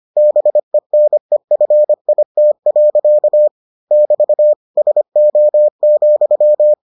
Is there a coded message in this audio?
Yes: Morse code from 0.2 to 6.7 s, heard as a keyed tone, about 25 wpm, 600 Hz. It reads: BENEFIT. <BT> SO,